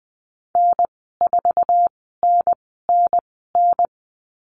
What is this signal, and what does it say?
Morse code, heard as a keyed tone, about 20 wpm, spaced as for 15 wpm, 700 Hz. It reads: N4NNN
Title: 048 Lesson 9g - Number 4 - Call Signs 20wpm